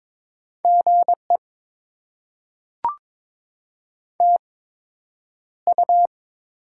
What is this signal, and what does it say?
Morse code, heard as a keyed tone, about 22 wpm, spaced as for 9 wpm, 700 Hz. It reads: GE TU
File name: Top 20 QSO Elements - 8x Character Spacing - 22wpm